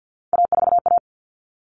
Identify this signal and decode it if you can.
Morse code, heard as a keyed tone, about 50 wpm, 700 Hz. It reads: A4A
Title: All Cut Numbers - 3-Digit 50wpm